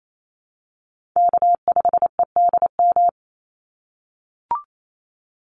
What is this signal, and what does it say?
Morse code, heard as a keyed tone, about 28 wpm, 700 Hz. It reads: K5EDM
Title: Call Signs - 28wpm